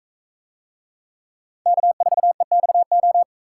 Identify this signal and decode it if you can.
Morse code, heard as a keyed tone, about 42 wpm, 700 Hz. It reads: K4EXO